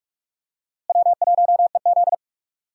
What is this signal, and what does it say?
Morse code, heard as a keyed tone, about 46 wpm, 700 Hz. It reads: W1EZ
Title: USA Call Signs - 4 Characters - Repeated 2x - 45wpm